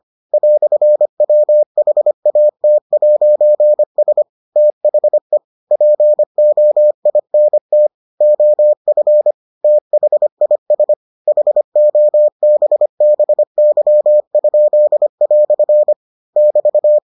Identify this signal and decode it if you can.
Morse code, heard as a keyed tone, about 25 wpm, 600 Hz. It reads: "WHAT'S THE POINT OF THIS HOBBY?" <BT>